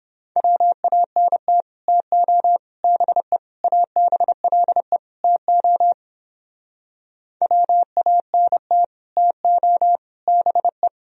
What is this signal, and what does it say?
Morse code, heard as a keyed tone, about 29 wpm, 700 Hz. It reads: WANT TO BE ABLE TO WANT TO BE